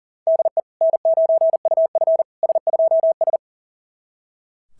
Morse code, heard as a keyed tone, about 38 wpm, 690 Hz. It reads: DE N9UF S2S
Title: SOTA QSO Practice 40wpm